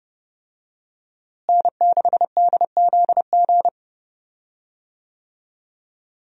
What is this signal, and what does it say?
Morse code, heard as a keyed tone, about 30 wpm, 700 Hz. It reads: N6DZG
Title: USA Call Signs - 5 Characters - Repeated 3x - 30wpm